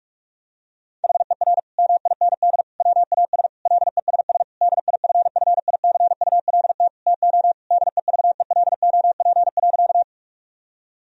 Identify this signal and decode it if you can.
Morse code, heard as a keyed tone, about 46 wpm, 700 Hz. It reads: HER MIND WAS LESS DIFFICULT TO DEVELOP.
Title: Sentences from Top 1000 Words - Review Part 2 - 45wpm